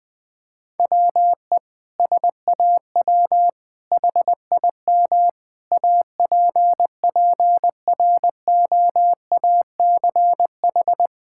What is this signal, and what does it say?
Morse code, heard as a keyed tone, about 21 wpm, 700 Hz. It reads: WE SAW HIM APPROACH
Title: Sentences from Top 1000 Words - Review Part 6 - 20wpm